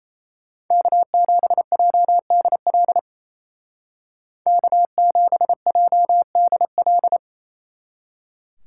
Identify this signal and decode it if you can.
Morse code, heard as a keyed tone, about 37 wpm, 700 Hz. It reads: K7JDL K7JDL